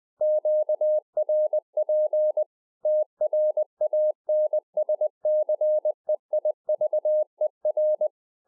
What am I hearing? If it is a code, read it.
Morse code, heard as a keyed tone, about 20 wpm, 610 Hz. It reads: QRP TRANSCEIVER
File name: 2011-04-mp3 / A SMALL TRANCEIVER 3 20WPM